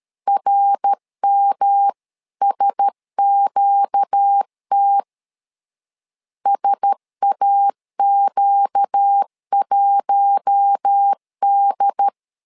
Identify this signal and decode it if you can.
Morse code, heard as a keyed tone, about 13 wpm, 800 Hz. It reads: RMSQT SAQ1D
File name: LEZIONI / 4 Q1S 12wpm 1802